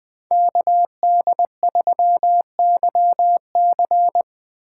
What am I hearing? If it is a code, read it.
Morse code, heard as a keyed tone, about 23 wpm, 700 Hz. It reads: KD3YC
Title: Speed Racing - USA 2x2 Call Signs 20wpm